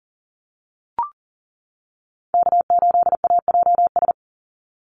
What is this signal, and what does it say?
Morse code, heard as a keyed tone, about 37 wpm, 700 Hz. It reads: K8AJS
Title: Call Signs - 40wpm